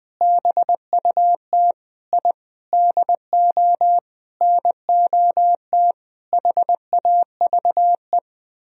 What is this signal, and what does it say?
Morse code, heard as a keyed tone, about 21 wpm, 700 Hz. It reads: BUT I DO NOT HAVE
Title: Sentences from Top 1000 Words - Review Part 3 - 20wpm